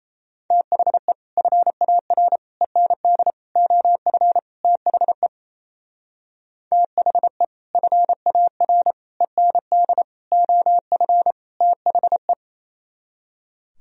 Morse code, heard as a keyed tone, about 34 wpm, 700 Hz. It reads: THE FAR END OF THE THE FAR END OF THE
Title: Speed-Racing - Sets of 5 Words 28wpm